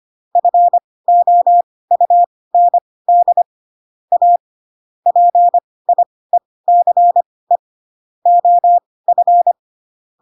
Morse code, heard as a keyed tone, about 25 wpm, spaced as for 18 wpm, 700 Hz. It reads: FOUND A PIECE OF
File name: Pinocchio / Pinocchio ch1 12 25 ik1hge